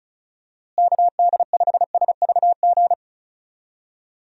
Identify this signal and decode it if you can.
Morse code, heard as a keyed tone, about 34 wpm, 700 Hz. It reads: KD5SVG
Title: USA Call Signs - 6 Characters - Repeated 2x - 35wpm